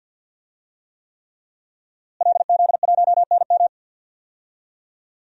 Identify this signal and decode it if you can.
Morse code, heard as a keyed tone, about 46 wpm, 700 Hz. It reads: LZ1NM
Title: DXCC-100 Call Signs - 5 Characters - Repeated 3x - 50wpm